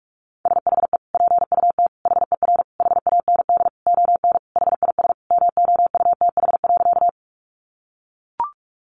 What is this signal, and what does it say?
Morse code, heard as a keyed tone, about 44 wpm, 700 Hz. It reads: SHE PUT HER HAND ON HIS MOUTH.
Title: Sentences from Top 1500 Words 45wpm